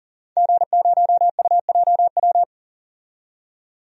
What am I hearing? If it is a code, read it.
Morse code, heard as a keyed tone, about 37 wpm, 700 Hz. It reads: G0UJW
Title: DXCC-100 Call Signs - 5 Characters - 40wpm